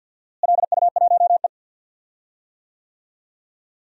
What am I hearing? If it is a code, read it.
Morse code, heard as a keyed tone, about 51 wpm, 700 Hz. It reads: LU1E
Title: DXCC-100 Call Signs - 4 Characters - Repeated 3x - 50wpm